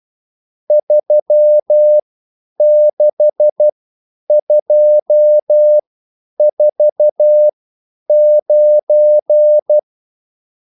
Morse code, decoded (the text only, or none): 36249